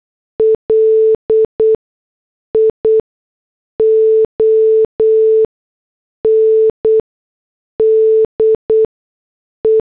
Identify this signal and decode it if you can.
Morse code, heard as a keyed tone, about 8 wpm, spaced as for 6 wpm, 440 Hz. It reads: LION DE
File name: morse / headlines